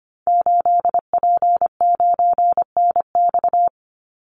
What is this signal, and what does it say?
Morse code, heard as a keyed tone, about 25 wpm, 700 Hz. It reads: 8P9NX
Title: Call Signs - 25wpm